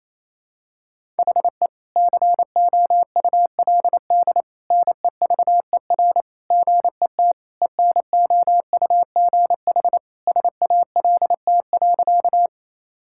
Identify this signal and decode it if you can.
Morse code, heard as a keyed tone, about 28 wpm, 700 Hz. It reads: HE COULD NEVER GET ENOUGH SALT.
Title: Sentences from Top 1000 Words - Review Part 5 - 28wpm